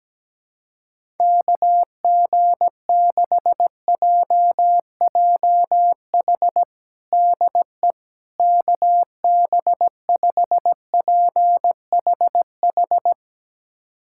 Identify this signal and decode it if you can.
Morse code, heard as a keyed tone, about 17 wpm, 700 Hz. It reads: KG6JJH DE KB5PHH
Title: USA Callsign DE Callsign 17wpm